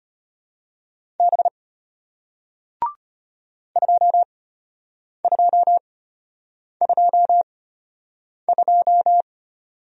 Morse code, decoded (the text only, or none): B 2222